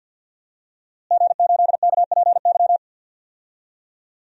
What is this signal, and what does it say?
Morse code, heard as a keyed tone, about 51 wpm, 700 Hz. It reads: G8KPY
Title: DXCC-100 Call Signs - 5 Characters - Repeated 2x - 50wpm